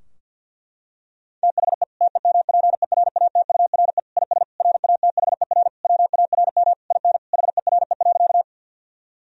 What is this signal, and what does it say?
Morse code, heard as a keyed tone, about 51 wpm, 700 Hz. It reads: THE TEMPERATURE IS RATHER WARM IN HERE.
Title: Sentences from Top 1000 Words - Review Part 4 - 50wpm